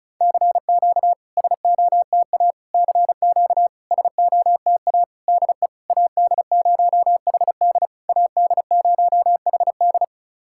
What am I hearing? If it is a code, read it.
Morse code, heard as a keyed tone, about 34 wpm, 690 Hz. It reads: CQ SOTA CQ SOTA DE AD0HD AD0HD
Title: SOTA QSO Practice 35wpm